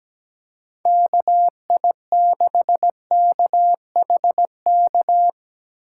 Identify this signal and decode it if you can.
Morse code, heard as a keyed tone, about 17 wpm, 700 Hz. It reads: KI6KHK
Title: USA Call Signs - 6 Characters - 4x Character Spacing - 17wpm